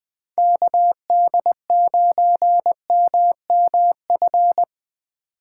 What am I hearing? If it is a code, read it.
Morse code, heard as a keyed tone, about 20 wpm, 700 Hz. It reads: KD9MMF
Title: Speed Racing - USA 2x3 Call Signs 17wpm